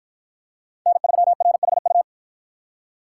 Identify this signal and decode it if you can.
Morse code, heard as a keyed tone, about 47 wpm, 700 Hz. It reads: N3RHU